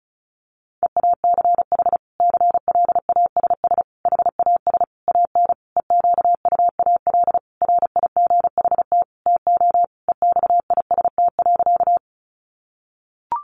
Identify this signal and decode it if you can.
Morse code, heard as a keyed tone, about 34 wpm, 700 Hz. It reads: EACH CLASS HAS AN EQUAL RIGHT TO EXIST.
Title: Sentences from Top 1400 Words 35wpm